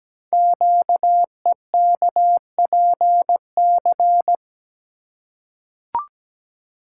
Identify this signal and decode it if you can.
Morse code, heard as a keyed tone, about 17 wpm, 700 Hz. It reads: QEKPC
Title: Code Groups - 5 Letters - 17wpm